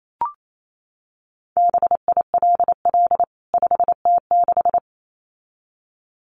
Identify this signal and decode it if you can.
Morse code, heard as a keyed tone, about 28 wpm, 700 Hz. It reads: BILL 5T6
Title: CWT Exchange 28wpm